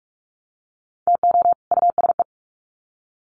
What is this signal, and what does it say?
Morse code, heard as a keyed tone, about 44 wpm, 700 Hz. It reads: TO USE
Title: Sets of 2 Words 45wpm